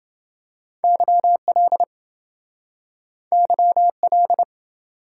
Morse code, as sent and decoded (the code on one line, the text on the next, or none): -.-- .-.. / -.-- .-..
YL YL